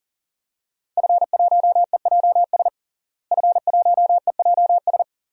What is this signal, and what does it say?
Morse code, heard as a keyed tone, about 40 wpm, 700 Hz. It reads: F1EJS F1EJS
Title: DXCC-100 Call Signs - 5 Characters - Repeated 3x - 40wpm